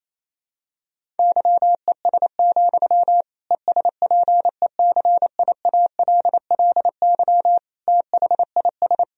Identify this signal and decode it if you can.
Morse code, heard as a keyed tone, about 28 wpm, 700 Hz. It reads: YES, ESPECIALLY THIS
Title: Sentences from Top 1000 Words - Review Part 4 - 28wpm